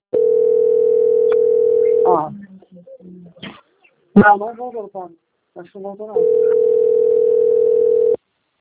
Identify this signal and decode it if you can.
Ringback tone (North America), 440 + 480 Hz, with speech.
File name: out- - - - - .32986